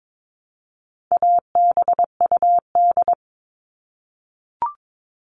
Morse code, heard as a keyed tone, about 22 wpm, 700 Hz. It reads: ABUD
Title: All Cut Numbers - 4-Digit 22wpm